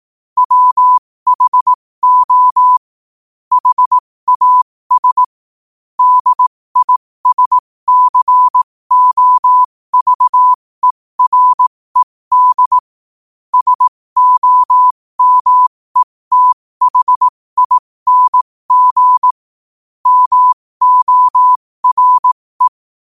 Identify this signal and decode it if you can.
Morse code, heard as a keyed tone, about 18 wpm, spaced as for 15 wpm, 1000 Hz. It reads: WHO HAS DISCOVERED SOMETHING MORE
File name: qotdfiles / 15wpm